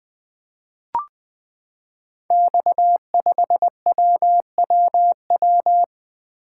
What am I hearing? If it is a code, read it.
Morse code, heard as a keyed tone, about 20 wpm, 700 Hz. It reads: X5WWW
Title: Code Groups - 5 Letter:Numbers - 20wpm